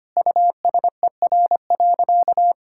Morse code, heard as a keyed tone, about 25 wpm, 700 Hz. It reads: USER.
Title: Sentences from Top 1000 Words - Review Part 5 - 25wpm